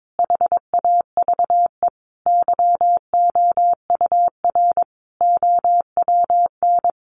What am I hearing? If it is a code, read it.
Morse code, heard as a keyed tone, about 22 wpm, 700 Hz. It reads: HAVE YOUR OWN
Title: Sentences from Top 1200 Words 22wpm